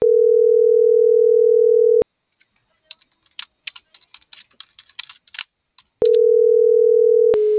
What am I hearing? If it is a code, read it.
Ringback tone (North America), 440 + 480 Hz.